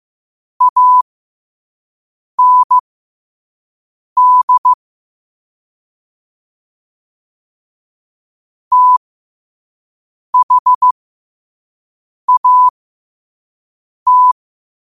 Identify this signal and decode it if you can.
Morse code, heard as a keyed tone, about 15 wpm, spaced as for 5 wpm, 1000 Hz. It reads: AND THAT
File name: qotdfiles / 5wpm